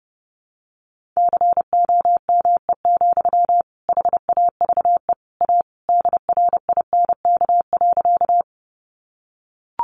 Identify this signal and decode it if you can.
Morse code, heard as a keyed tone, about 30 wpm, 700 Hz. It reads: COME, HAVE A DRINK.
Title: Sentences from Top 1500 Words 30wpm